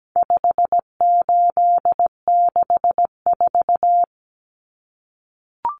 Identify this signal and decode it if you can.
Morse code, heard as a keyed tone, about 17 wpm, 700 Hz. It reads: 5864
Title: Numbers - 4-Digit 17wpm